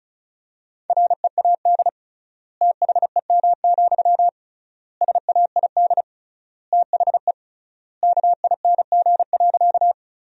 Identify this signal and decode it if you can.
Morse code, heard as a keyed tone, about 36 wpm, 700 Hz. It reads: READ THEM, SAID THE KING.
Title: 3x Word-Spacing - Sentences from Top 400 Words - 35wpm